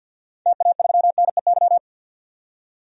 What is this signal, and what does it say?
Morse code, heard as a keyed tone, about 52 wpm, 700 Hz. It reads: TA3NEY
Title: DXCC-100 Call Signs - 6 Characters - 50wpm